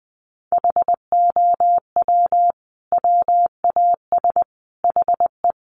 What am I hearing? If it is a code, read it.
Morse code, heard as a keyed tone, about 20 wpm, 700 Hz. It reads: HOW WAS HE
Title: Sentences from Top 1400 Words 20wpm